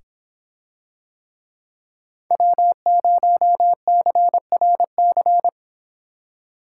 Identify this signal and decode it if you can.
Morse code, heard as a keyed tone, about 26 wpm, 700 Hz. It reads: W0CRC